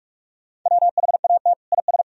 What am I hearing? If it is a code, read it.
Morse code, heard as a keyed tone, about 46 wpm, 700 Hz. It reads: WHAT IS
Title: Sentences from Top 1000 Words - Review Part 1 - 45wpm